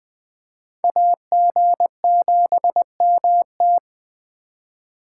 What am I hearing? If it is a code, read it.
Morse code, heard as a keyed tone, about 21 wpm, 700 Hz. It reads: AG7MT